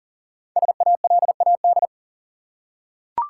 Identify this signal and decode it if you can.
Morse code, heard as a keyed tone, about 39 wpm, 700 Hz. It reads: SALAD
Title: Five Letter Words - 40wpm